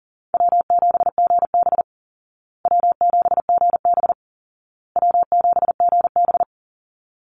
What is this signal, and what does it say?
Morse code, heard as a keyed tone, about 40 wpm, 700 Hz. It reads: W7GB W7GB W7GB
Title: Easy Call Signs Repeated 3x - 40wpm